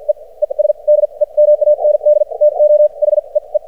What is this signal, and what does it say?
Morse code, heard as a keyed tone, about 56 wpm, spaced as for 38 wpm, 600 Hz. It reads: EES DEMAND TO HEE
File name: archived / CRSnet